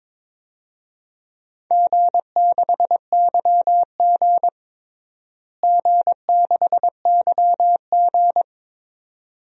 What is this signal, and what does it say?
Morse code, heard as a keyed tone, about 21 wpm, 700 Hz. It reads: G6YG G6YG
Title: DXCC-100 Call Signs - 4 Characters - Repeated 2x - 22wpm